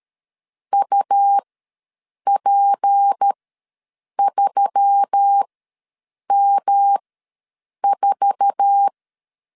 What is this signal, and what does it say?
Morse code, heard as a keyed tone, about 13 wpm, spaced as for 7 wpm, 800 Hz. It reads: UP3M4
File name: LEZIONI / 8 YJ45 / 8 YJ4 ws 12wpm 3249